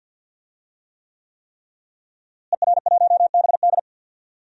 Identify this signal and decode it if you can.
Morse code, heard as a keyed tone, about 46 wpm, 700 Hz. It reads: ER1BD
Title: DXCC-100 Call Signs - 5 Characters - Repeated 3x - 50wpm